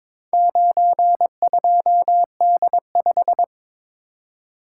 Morse code, heard as a keyed tone, about 22 wpm, 700 Hz. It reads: 92D5
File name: practice / 22wpm / Four Letter-Number Combinations - 22wpm